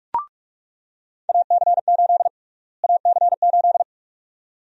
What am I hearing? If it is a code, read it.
Morse code, heard as a keyed tone, about 46 wpm, 700 Hz. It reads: AC8 AC8
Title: USA Call Sign Prefixes - Repeated 2x - 45wpm